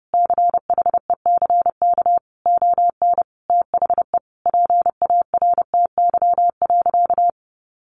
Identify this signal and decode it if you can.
Morse code, heard as a keyed tone, about 30 wpm, 700 Hz. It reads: CHECK ON THE PARTY.
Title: Sentences from Top 1400 Words 30wpm